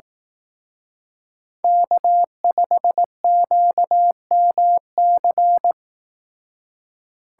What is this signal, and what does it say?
Morse code, heard as a keyed tone, about 18 wpm, 700 Hz. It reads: K5QMC
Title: Speed Racing - USA 1x3 Call Signs 15wpm